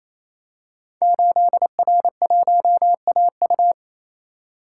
Morse code, heard as a keyed tone, about 28 wpm, 700 Hz. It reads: 8R1AU
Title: Code Groups - 5 Letter:Numbers - 28wpm